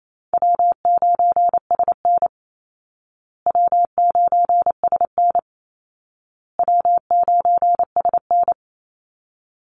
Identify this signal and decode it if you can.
Morse code, heard as a keyed tone, about 28 wpm, 700 Hz. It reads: W9SN W9SN W9SN
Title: Easy Call Signs Repeated 3x - 28wpm